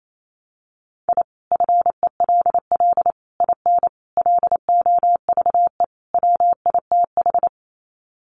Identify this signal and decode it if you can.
Morse code, heard as a keyed tone, about 28 wpm, 700 Hz. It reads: I FELL IN LOVE WITH
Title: Sets of 5 Words Encore 28wpm